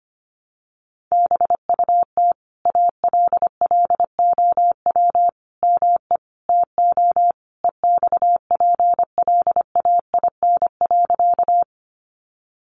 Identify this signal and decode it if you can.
Morse code, heard as a keyed tone, about 25 wpm, 700 Hz. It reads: BUT ALLOW ME TO EXPLAIN.
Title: Sentences from Top 1100 Words 25wpm